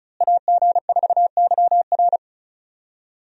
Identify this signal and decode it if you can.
Morse code, heard as a keyed tone, about 34 wpm, 700 Hz. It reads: AG4YR
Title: USA Call Signs - 5 Characters - Repeated 2x - 35wpm